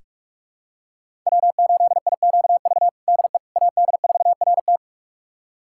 Morse code, heard as a keyed tone, about 42 wpm, 700 Hz. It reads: W8IQV DE AD4RT
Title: USA Callsign DE Callsign 45wpm